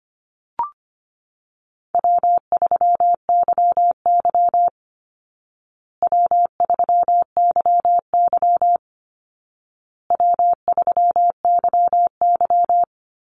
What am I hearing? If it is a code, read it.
Morse code, heard as a keyed tone, about 25 wpm, 700 Hz. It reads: W3YY W3YY W3YY
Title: Call Signs Repeated 2x - 25wpm